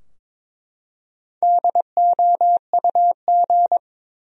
Morse code, heard as a keyed tone, about 22 wpm, 700 Hz. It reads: DOUG
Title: USA Common Names - Max 4 Letters - 22wpm